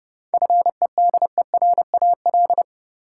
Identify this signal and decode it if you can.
Morse code, heard as a keyed tone, about 30 wpm, 700 Hz. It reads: FEDERAL
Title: Sentences from Top 1000 Words - Review Part 5 - New Word Drill - 30wpm